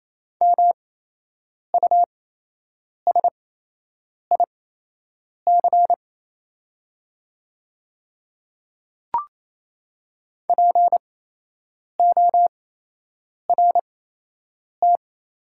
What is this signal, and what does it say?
Morse code, heard as a keyed tone, about 27 wpm, spaced as for 8 wpm, 690 Hz. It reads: MUSIC PORT
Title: 8x Character Spacing - Top 500 Words - 28wpm